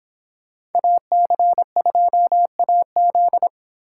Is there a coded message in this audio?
Yes: Morse code from 0.8 to 3.5 s, heard as a keyed tone, about 24 wpm, 700 Hz. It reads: AC2AZ